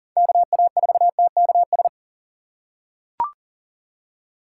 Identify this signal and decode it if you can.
Morse code, heard as a keyed tone, about 40 wpm, 700 Hz. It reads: KA4TKS
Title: USA Call Signs - 6 Characters - 2x Character Spacing - 40wpm